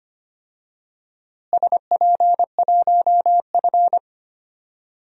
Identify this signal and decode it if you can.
Morse code, heard as a keyed tone, about 25 wpm, 700 Hz. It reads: SP1F
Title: DXCC-100 Call Signs - 4 Characters - Repeated 2x - 25wpm